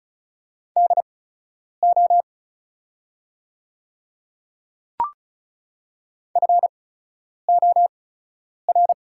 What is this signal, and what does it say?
Morse code, heard as a keyed tone, about 34 wpm, spaced as for 10 wpm, 690 Hz. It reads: DO FOR
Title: 8x Character Spacing - Top 100 Words - 35wpm